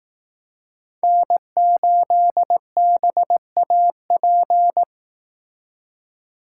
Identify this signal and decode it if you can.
Morse code, heard as a keyed tone, about 18 wpm, 700 Hz. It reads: N8BAP
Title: Speed Racing - USA 1x3 Call Signs 15wpm